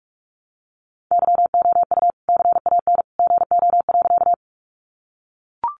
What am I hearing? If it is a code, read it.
Morse code, heard as a keyed tone, about 44 wpm, 700 Hz. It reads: YOU CAN GO.
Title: Sentences from Top 100 Words 45wpm